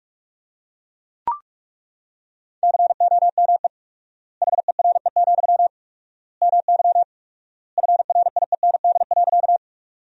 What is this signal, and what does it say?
Morse code, heard as a keyed tone, about 41 wpm, 700 Hz. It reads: COME HERE, MY FRIEND.
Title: 4x Word-Spacing - Sentences from Top 300 Words - 45wpm